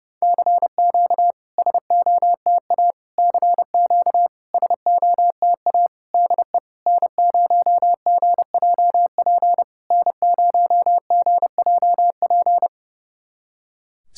Morse code, heard as a keyed tone, about 30 wpm, 700 Hz. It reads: CQ SOTA CQ SOTA DE N0GJP N0GJP
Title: SOTA QSO Practice 30wpm